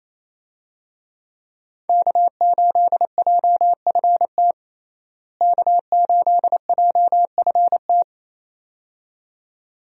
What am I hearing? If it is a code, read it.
Morse code, heard as a keyed tone, about 28 wpm, 700 Hz. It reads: K8JFT K8JFT
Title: USA Call Signs - 5 Characters - Repeated 2x - 28wpm